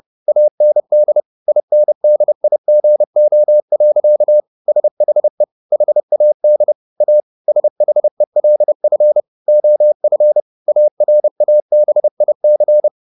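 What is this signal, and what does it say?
Morse code, heard as a keyed tone, about 30 wpm, 600 Hz. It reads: AND INDIGO. SHE HAD A SHELF OF ARABIC
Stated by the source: Morse code at 30 wpm, the novel Never by Ken Follett in English, from CW med Gnister.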